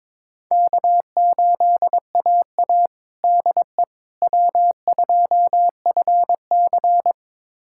Morse code, heard as a keyed tone, about 22 wpm, 700 Hz. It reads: K8AA DE W2FC
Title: USA Callsign DE Callsign - 4 Characters - 22wpm